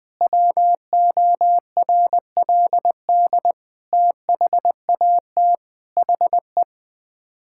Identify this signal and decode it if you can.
Morse code, heard as a keyed tone, about 21 wpm, 700 Hz. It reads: WORLD THAT HE
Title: Sets of 8 Words - 20wpm